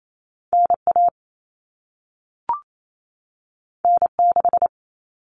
Morse code, heard as a keyed tone, about 28 wpm, 700 Hz. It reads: NA N6
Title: Common Cut Numbers - 2-Digit 28wpm